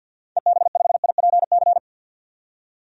Morse code, heard as a keyed tone, about 46 wpm, 700 Hz. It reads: EB5IPC
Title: DXCC-100 Call Signs - 6 Characters - 50wpm